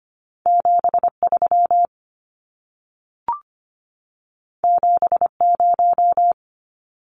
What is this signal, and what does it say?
Morse code, heard as a keyed tone, about 25 wpm, 700 Hz. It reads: 73 70
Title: Numbers - 2-Digit 25wpm